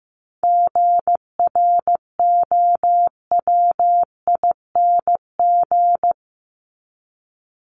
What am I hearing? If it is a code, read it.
Morse code, heard as a keyed tone, about 15 wpm, 700 Hz. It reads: GROWING
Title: ING Suffix - Top 100 Words with Base Words 15wpm